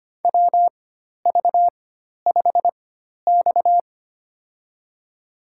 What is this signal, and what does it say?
Morse code, heard as a keyed tone, about 25 wpm, spaced as for 12 wpm, 700 Hz. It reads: WV5X
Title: USA Call Signs - 4x Character Spacing - 25wpm